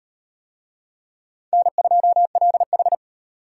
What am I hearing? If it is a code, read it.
Morse code, heard as a keyed tone, about 38 wpm, 700 Hz. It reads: N2LH